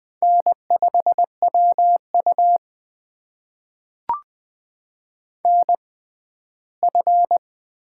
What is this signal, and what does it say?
Morse code, heard as a keyed tone, about 20 wpm, spaced as for 14 wpm, 700 Hz. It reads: N5WU NF
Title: USA Call Signs - 4 Characters - 6x Character Spacing - 20wpm